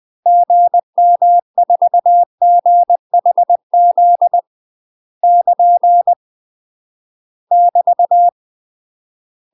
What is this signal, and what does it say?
Morse code, heard as a keyed tone, about 20 wpm, 700 Hz. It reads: GM4GHZ <KN> <BT>
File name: UK MASTER QSOs TRAINING / 20WPM / UK MASTER QSOs 0004